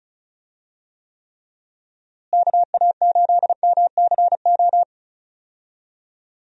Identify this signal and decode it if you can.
Morse code, heard as a keyed tone, about 34 wpm, 700 Hz. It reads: KA8MCO
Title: USA Call Signs - 6 Characters - Repeated 3x - 35wpm